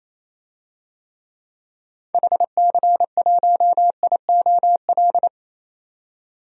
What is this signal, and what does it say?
Morse code, heard as a keyed tone, about 28 wpm, 700 Hz. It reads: HC1IOL
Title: DXCC-100 Call Signs - 6 Characters - 28wpm